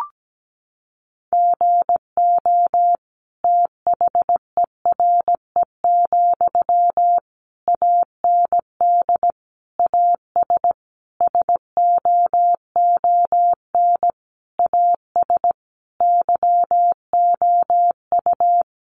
Morse code, heard as a keyed tone, about 17 wpm, 700 Hz. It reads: GO THERE, AND AS SOON AS YOU
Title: Sentences from Top 1300 Words 17wpm